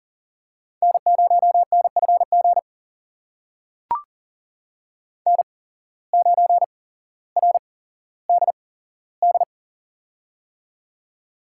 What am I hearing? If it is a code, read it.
Morse code, heard as a keyed tone, about 40 wpm, spaced as for 17 wpm, 700 Hz. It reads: N0NFG N9RDD
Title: USA Call Signs - 5 Characters - 8x Character Spacing - 40wpm